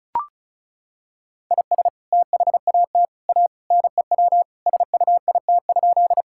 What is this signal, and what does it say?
Morse code, heard as a keyed tone, about 36 wpm, 700 Hz. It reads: IS THAT A NEW SUIT?
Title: Sentences from Top 1600 Words 35wpm